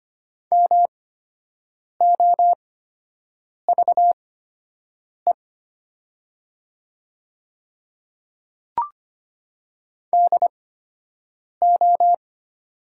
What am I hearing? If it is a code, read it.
Morse code, heard as a keyed tone, about 25 wpm, spaced as for 7 wpm, 700 Hz. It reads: MOVE DO
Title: 8x Character Spacing - Top 200 Words - 25wpm